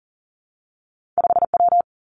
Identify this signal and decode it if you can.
Morse code, heard as a keyed tone, about 40 wpm, 700 Hz. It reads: 5 W
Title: Two Challenging Letter-Number Combinations 40wpm